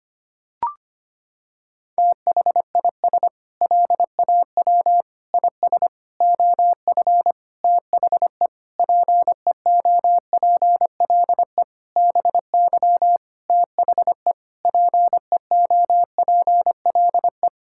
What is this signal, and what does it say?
Morse code, heard as a keyed tone, about 25 wpm, 700 Hz. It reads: THIS LAW IS OF THE PEOPLE BY THE PEOPLE
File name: practice / 25wpm / Sentences with 9-Word Phrases - 25wpm